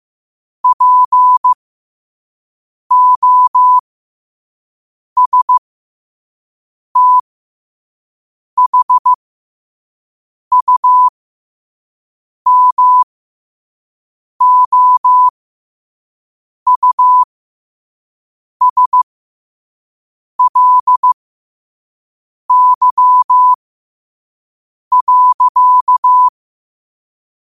Morse code, heard as a keyed tone, about 15 wpm, spaced as for 5 wpm, 1000 Hz. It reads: POSTHUMOUSLY.
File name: qotdfiles / 5wpm